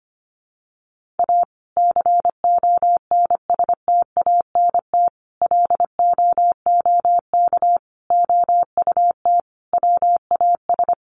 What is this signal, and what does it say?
Morse code, heard as a keyed tone, about 25 wpm, 700 Hz. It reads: A CONSTANT LOOK OUT WAS
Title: Sentences from Top 1500 Words 25wpm